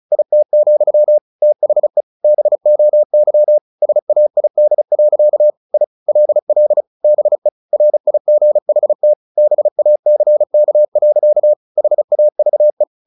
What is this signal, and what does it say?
Morse code, heard as a keyed tone, about 34 wpm, 600 Hz. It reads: IT, THE BOY SAID. I LL BE RIGHT BACK. HAVE
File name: the old man and the sea / the old man and the sea 1 35wpm